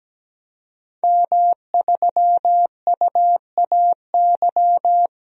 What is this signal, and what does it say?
Morse code, heard as a keyed tone, about 17 wpm, 700 Hz. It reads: M3UAY
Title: DXCC-100 Call Signs - 5 Characters - 17wpm